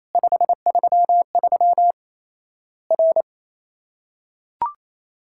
Morse code, heard as a keyed tone, about 28 wpm, 700 Hz. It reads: 533 R
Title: Worked All Europe DX Contest - QTC Exchange - 28wpm